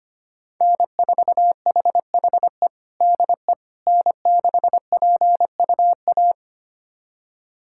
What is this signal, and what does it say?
Morse code, heard as a keyed tone, about 25 wpm, 700 Hz. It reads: N4HHE DE N6PUA
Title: USA Callsign DE Callsign - 5 Characters - 25wpm